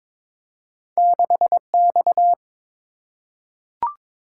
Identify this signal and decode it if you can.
Morse code, heard as a keyed tone, about 22 wpm, 700 Hz. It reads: 6X
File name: practice / 22wpm / Two Letter-Number Combinations - 22wpm